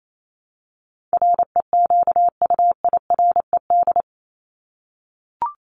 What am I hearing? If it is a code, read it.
Morse code, heard as a keyed tone, about 28 wpm, 700 Hz. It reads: REQUIRED
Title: ED Suffix - Top 100 Words with Base Words 28wpm